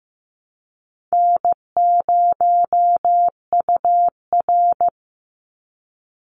Morse code, heard as a keyed tone, about 15 wpm, 700 Hz. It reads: N0UR
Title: Easy Call Signs Repeated 2x - 15wpm